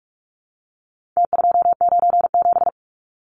Morse code, heard as a keyed tone, about 44 wpm, 700 Hz. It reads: T297
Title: Numbers - Contest Style - 4-Digit 45wpm